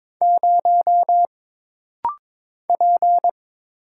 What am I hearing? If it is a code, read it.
Morse code, heard as a keyed tone, about 22 wpm, 700 Hz. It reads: 0 P
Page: This rapid-fire timing is geared to push your ICR (Instant Character Recognition) skills.